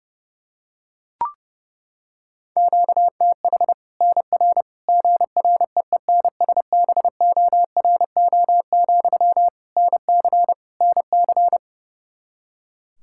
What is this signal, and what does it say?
Morse code, heard as a keyed tone, about 32 wpm, 700 Hz. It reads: QTH NR GREENSBORO, NC NC
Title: Practice QSO - Exchange 1 - Beginning - 30wpm